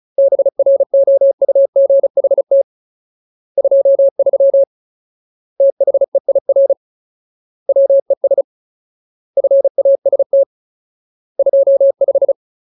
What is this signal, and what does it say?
Morse code, heard as a keyed tone, about 34 wpm, 550 Hz. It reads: BROUGHT 23 THEIR WES FAST 25
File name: Ultimate Word List / 35WPM Ultimate